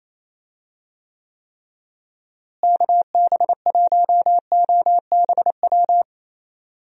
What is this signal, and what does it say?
Morse code, heard as a keyed tone, about 28 wpm, 700 Hz. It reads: KB1OBW
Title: USA Call Signs - 6 Characters - Repeated 3x - 28wpm